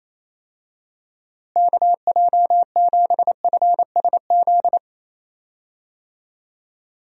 Morse code, heard as a keyed tone, about 28 wpm, 700 Hz. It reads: KJ7FSZ
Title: USA Call Signs - 6 Characters - Repeated 2x - 28wpm